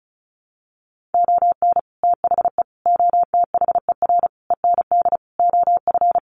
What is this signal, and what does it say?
Morse code, heard as a keyed tone, about 34 wpm, 700 Hz. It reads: ON THE OTHER END OF
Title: Sets of 5 Words Encore 35wpm